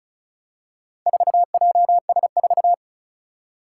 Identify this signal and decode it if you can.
Morse code, heard as a keyed tone, about 36 wpm, 690 Hz. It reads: 4JS4
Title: Four Challenging Letter-Number Combinations 35wpm